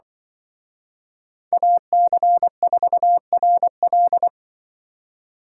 Morse code, heard as a keyed tone, about 23 wpm, 700 Hz. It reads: AC4RL
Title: Speed Racing - USA 2x2 Call Signs 20wpm